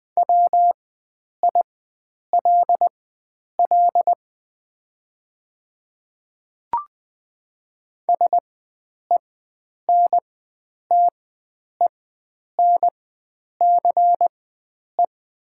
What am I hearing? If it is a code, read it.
Morse code, heard as a keyed tone, about 20 wpm, spaced as for 9 wpm, 700 Hz. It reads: WILL SENTENCE
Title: 4x Character Spacing - Top 300 Words - 20wpm